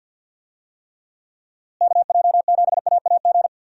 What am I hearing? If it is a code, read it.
Morse code, heard as a keyed tone, about 51 wpm, 700 Hz. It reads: KJ7AAG
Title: USA Call Signs - 6 Characters - Repeated 2x - 50wpm